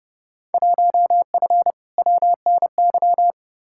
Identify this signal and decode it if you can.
Morse code, heard as a keyed tone, about 32 wpm, 700 Hz. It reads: 1F WNY
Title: Field Day Exchanges - 30wpm